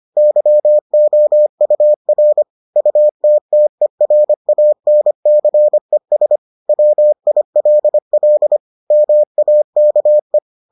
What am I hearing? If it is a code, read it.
Morse code, heard as a keyed tone, about 25 wpm, 600 Hz. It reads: YOUR UTTERANCES WILL MAKE